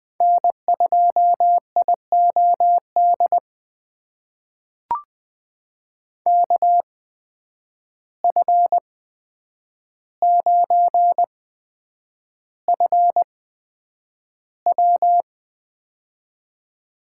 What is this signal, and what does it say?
Morse code, heard as a keyed tone, about 21 wpm, spaced as for 9 wpm, 700 Hz. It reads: N2IOD KF9FW
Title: USA Call Signs - 5 Characters - 8x Character Spacing - 20wpm